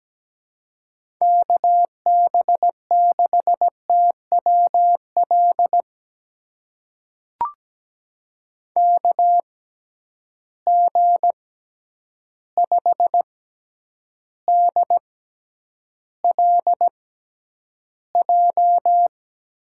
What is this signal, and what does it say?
Morse code, heard as a keyed tone, about 17 wpm, spaced as for 9 wpm, 700 Hz. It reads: KB6TWL KG5DLJ
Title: USA Call Signs - 6 Characters - 6x Character Spacing - 17wpm